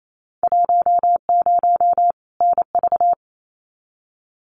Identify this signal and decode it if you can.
Morse code, heard as a keyed tone, about 28 wpm, 700 Hz. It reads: 10 NV
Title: Winter Field Day Exchanges 28wpm